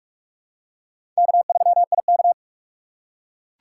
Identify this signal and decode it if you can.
Morse code, heard as a keyed tone, about 44 wpm, 690 Hz. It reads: K3IK
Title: USA Call Signs - 4 Characters - 3x Character Spacing - 45wpm